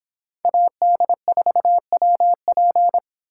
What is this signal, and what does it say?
Morse code, heard as a keyed tone, about 24 wpm, 700 Hz. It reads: AD4WP